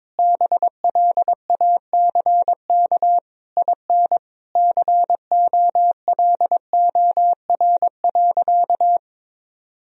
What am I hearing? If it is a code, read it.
Morse code, heard as a keyed tone, about 22 wpm, 700 Hz. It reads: BLACK IN COLOR.
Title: Sentences from Top 1000 Words - Review Part 2 - 22wpm